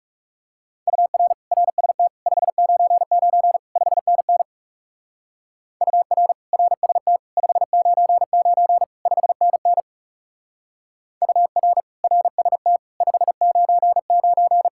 Morse code, heard as a keyed tone, about 35 wpm, 700 Hz. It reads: UR RST 599 5NN UR RST 599 5NN UR RST 599
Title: Speed-Racing Common QSO Phrases Encore - 30wpm